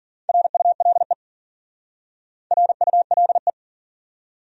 Morse code, heard as a keyed tone, about 47 wpm, 700 Hz. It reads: RULE RULE
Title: Speed-Racing Top 500 Words 40wpm